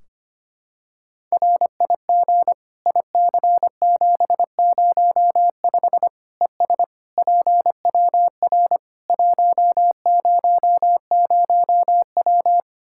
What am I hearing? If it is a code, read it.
Morse code, heard as a keyed tone, about 25 wpm, 700 Hz. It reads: RIG IC705 ES PWR 100W
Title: Practice QSO - Exchange 2 - Info - 25wpm